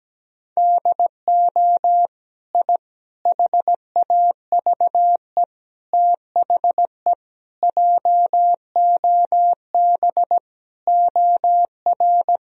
Morse code, heard as a keyed tone, about 17 wpm, 700 Hz. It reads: DO I HAVE THE JOB OR
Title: Sentences from Top 1000 Words - Review Part 3 - 17wpm